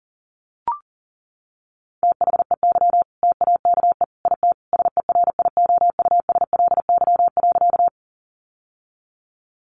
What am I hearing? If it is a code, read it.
Morse code, heard as a keyed tone, about 40 wpm, 700 Hz. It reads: THEY TAKE IT SERIOUSLY.
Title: Sentences from Top 1200 Words 40wpm